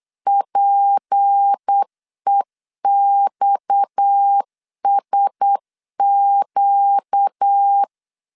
Morse code, heard as a keyed tone, about 8 wpm, 800 Hz. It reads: PEXSQ
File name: LEZIONI / 6 PKX0 / 6 PKX 8wpm 3041